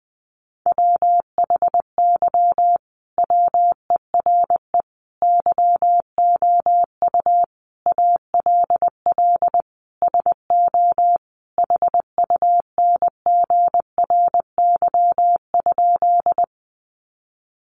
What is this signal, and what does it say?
Morse code, heard as a keyed tone, about 20 wpm, 700 Hz. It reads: WHY WERE YOU ALL SO HUNGRY?
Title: Sentences from Top 1400 Words 20wpm